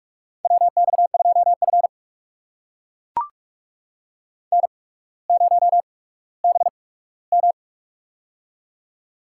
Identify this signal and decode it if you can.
Morse code, heard as a keyed tone, about 42 wpm, spaced as for 19 wpm, 700 Hz. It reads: WX2F N0BM
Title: USA Call Signs - 4 Characters - 8x Character Spacing - 45wpm